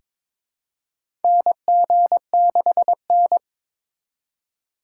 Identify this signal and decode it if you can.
Morse code, heard as a keyed tone, about 22 wpm, 700 Hz. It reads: NG6N
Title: USA Call Signs - 4 Characters - 3x Character Spacing - 22wpm